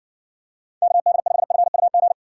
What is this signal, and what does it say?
Morse code, heard as a keyed tone, about 60 wpm, 700 Hz. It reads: KD5FUG